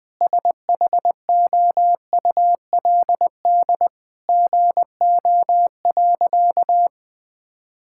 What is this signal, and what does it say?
Morse code, heard as a keyed tone, about 20 wpm, 700 Hz. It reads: SHOULD GO.
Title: Sentences with 3-Word Phrases - 20wpm